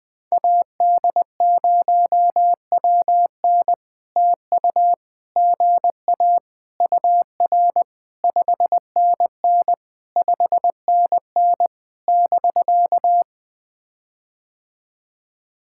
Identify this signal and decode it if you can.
Morse code, heard as a keyed tone, about 20 wpm, 700 Hz. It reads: AD0WN TU GA UR 5NN 5NN <BK>
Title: POTA QSO Practice - 20wpm